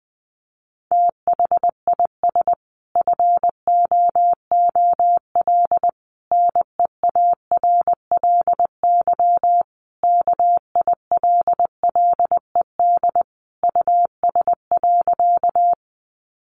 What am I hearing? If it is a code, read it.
Morse code, heard as a keyed tone, about 20 wpm, 700 Hz. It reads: THIS FOOL NEARLY KILLED US.
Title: A Sound Of Thunder 20wpm